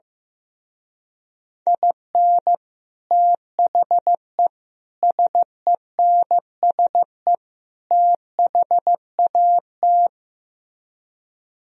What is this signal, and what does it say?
Morse code, heard as a keyed tone, about 15 wpm, 700 Hz. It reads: IN THE SENSE THAT
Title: Speed-Racing - Sets of 4 Words 15wpm